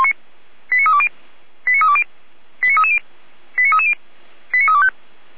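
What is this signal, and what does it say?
Five-tone selective call (ZVEI), heard as quick runs of five tones, tones of about 70 ms. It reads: 89320 89320 89330 89330 89325
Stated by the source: ZVEI five-tone call received by SDR, from the feuerwehr-orlamuende.org fire brigade server.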